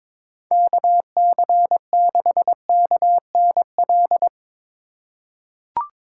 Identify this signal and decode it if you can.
Morse code, heard as a keyed tone, about 22 wpm, 700 Hz. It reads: KC6KNL
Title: USA Call Signs - 6 Characters - 3x Character Spacing - 22wpm